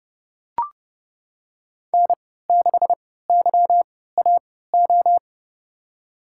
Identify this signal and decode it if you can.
Morse code, heard as a keyed tone, about 30 wpm, spaced as for 17 wpm, 700 Hz. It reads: N6YAO
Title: USA Call Signs - 5 Characters - 3x Character Spacing - 30wpm